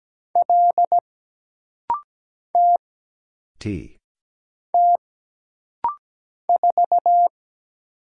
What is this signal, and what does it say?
Morse code, heard as a keyed tone, about 17 wpm, 700 Hz. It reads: L T T 4
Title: Single Letter-Number Alt - Rapid-Fire 17wpm